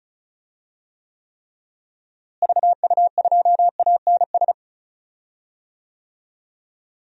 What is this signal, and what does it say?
Morse code, heard as a keyed tone, about 36 wpm, 700 Hz. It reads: VU2ANS
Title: DXCC-100 Call Signs - 6 Characters - Repeated 3x - 35wpm